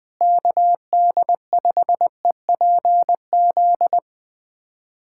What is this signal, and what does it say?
Morse code, heard as a keyed tone, about 20 wpm, 700 Hz. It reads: KD5EPZ
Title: USA Callsign DE Callsign - 6 Characters - 20wpm